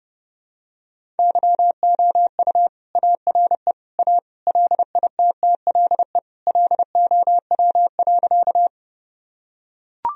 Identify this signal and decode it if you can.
Morse code, heard as a keyed tone, about 32 wpm, 700 Hz. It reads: YOU ARE A LITTLE LOW.
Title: Sentences from Top 1000 Words - Review Part 1 - 30wpm